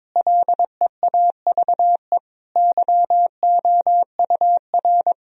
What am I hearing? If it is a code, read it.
Morse code, heard as a keyed tone, about 22 wpm, 700 Hz. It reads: LEAVE YOUR
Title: Sentences from Top 1000 Words - Review Part 3 - 22wpm